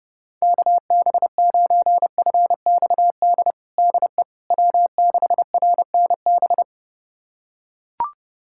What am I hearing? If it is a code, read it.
Morse code, heard as a keyed tone, about 30 wpm, 700 Hz. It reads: KB9FXD DE W6RNB
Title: USA Callsign DE Callsign 30wpm